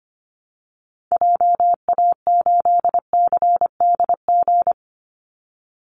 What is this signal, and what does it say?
Morse code, heard as a keyed tone, about 25 wpm, 700 Hz. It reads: JA8CDG
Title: Call Signs Repeated 2x - 25wpm